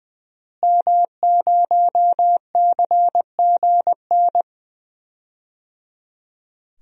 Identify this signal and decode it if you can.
Morse code, heard as a keyed tone, about 20 wpm, 700 Hz. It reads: M0CGN
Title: DXCC-100 Call Signs - 5 Characters - Repeated 3x - 20wpm